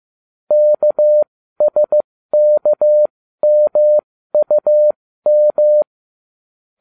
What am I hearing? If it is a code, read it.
Morse code, heard as a keyed tone, about 15 wpm, spaced as for 13 wpm, 600 Hz. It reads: KSKMUM